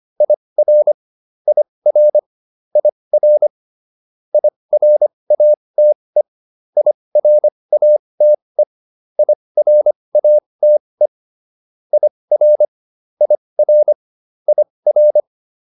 Morse code, heard as a keyed tone, about 25 wpm, spaced as for 20 wpm, 600 Hz. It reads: IR IR IR IRATE IRATE IRATE IR IR IR